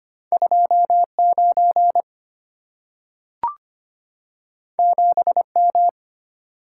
Morse code, heard as a keyed tone, about 25 wpm, 690 Hz. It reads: 29 7M
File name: Two Letter-Number Combinations - 25wpm